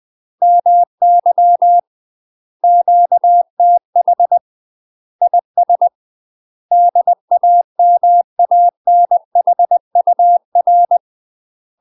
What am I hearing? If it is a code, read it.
Morse code, heard as a keyed tone, about 20 wpm, 700 Hz. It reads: MY QTH IS DAMANHUR
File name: UK MASTER QSOs TRAINING / 20WPM / UK MASTER QSOs 0060